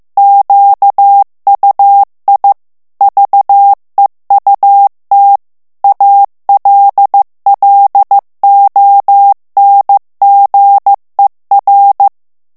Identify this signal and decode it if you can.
Morse code, heard as a keyed tone, about 15 wpm, 800 Hz. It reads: QUI VEUT ALLONGER
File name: morse mp3 / FRANCAIS TEXTE CLAIR 1 VITESSE 14 MOTS MINUTES